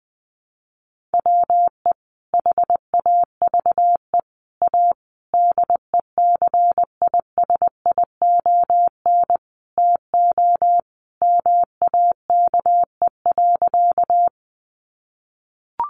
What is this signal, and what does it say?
Morse code, heard as a keyed tone, about 19 wpm, 700 Hz. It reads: WE HAVE A DECISION TO MAKE.
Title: Sentences from Top 1000 Words 20wpm